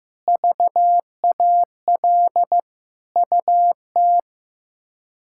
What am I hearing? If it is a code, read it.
Morse code, heard as a keyed tone, about 15 wpm, 700 Hz. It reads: VAL UT
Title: K1USN SST QSO Practice - 15wpm